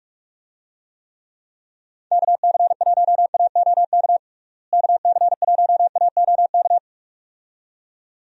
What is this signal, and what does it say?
Morse code, heard as a keyed tone, about 44 wpm, 700 Hz. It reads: KC1AOK KC1AOK
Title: USA Call Signs - 6 Characters - Repeated 2x - 45wpm